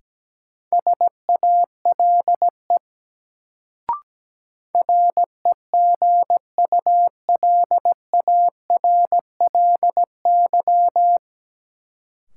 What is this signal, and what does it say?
Morse code, heard as a keyed tone, about 17 wpm, 700 Hz. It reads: SALE REGULARLY
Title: Sentences from Top 1400 Words - New Word Drill - 17wpm